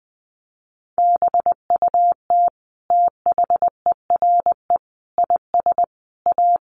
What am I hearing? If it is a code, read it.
Morse code, heard as a keyed tone, about 20 wpm, 700 Hz. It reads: BUT THERE IS A
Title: Sentences from Top 1100 Words 20wpm